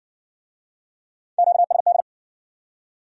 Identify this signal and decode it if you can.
Morse code, heard as a keyed tone, about 55 wpm, 700 Hz. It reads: QSD